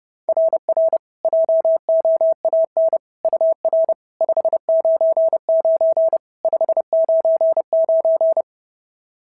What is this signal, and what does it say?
Morse code, heard as a keyed tone, about 30 wpm, 650 Hz. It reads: RR JOAN UR 599 599
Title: SOTA QSO Practice 30wpm